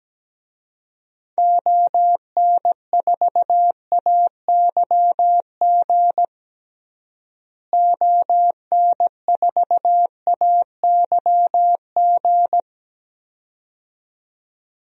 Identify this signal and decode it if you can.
Morse code, heard as a keyed tone, about 17 wpm, 700 Hz. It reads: ON4AYG ON4AYG
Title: DXCC-100 Call Signs - 6 Characters - Repeated 2x - 17wpm